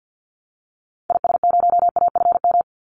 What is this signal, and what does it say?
Morse code, heard as a keyed tone, about 50 wpm, 700 Hz. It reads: IS0AFM
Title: Call Signs - 50wpm